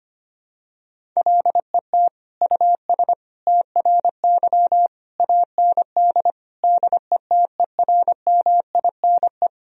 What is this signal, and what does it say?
Morse code, heard as a keyed tone, about 25 wpm, 700 Hz. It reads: LET US TRY AND DETERMINE
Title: Sentences from Top 1000 Words - Review Part 4 - 25wpm